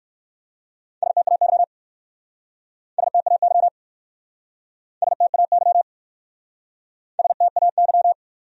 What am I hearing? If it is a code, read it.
Morse code, heard as a keyed tone, about 56 wpm, 700 Hz. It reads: STAY STAY STAY STAY
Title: Speed-Racing Top 500 Words 45wpm